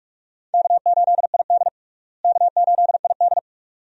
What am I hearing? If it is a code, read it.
Morse code, heard as a keyed tone, about 42 wpm, 700 Hz. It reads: K8ID K8ID
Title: USA Call Signs - 4 Characters - Repeated 2x - 45wpm